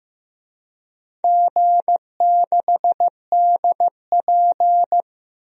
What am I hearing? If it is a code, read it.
Morse code, heard as a keyed tone, about 15 wpm, 700 Hz. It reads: G6DP
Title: DXCC-100 Call Signs - 4 Characters - 15wpm